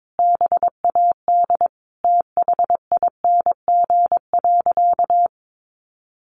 Morse code, heard as a keyed tone, about 22 wpm, 700 Hz. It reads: BAD THING.
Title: Sentences from Top 1200 Words 22wpm